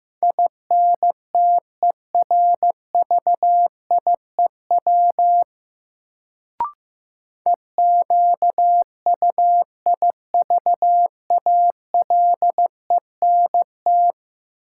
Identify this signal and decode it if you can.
Morse code, heard as a keyed tone, about 15 wpm, 700 Hz. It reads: INTERVIEW EQUIVALENT
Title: Sentences from Top 1600 Words - New Word Drill Alt - 15wpm